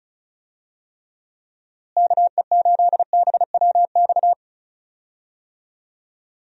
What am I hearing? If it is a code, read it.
Morse code, heard as a keyed tone, about 36 wpm, 700 Hz. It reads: KE8BWX
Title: USA Call Signs - 6 Characters - Repeated 3x - 35wpm